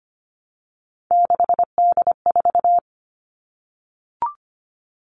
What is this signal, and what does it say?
Morse code, heard as a keyed tone, about 25 wpm, 700 Hz. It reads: 6D4
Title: Three Challenging Letter-Number Combinations 25wpm